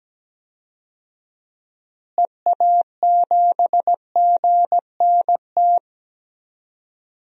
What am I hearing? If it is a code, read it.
Morse code, heard as a keyed tone, about 17 wpm, 700 Hz. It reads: EA7GNT
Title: DXCC-100 Call Signs - 6 Characters - Repeated 3x - 17wpm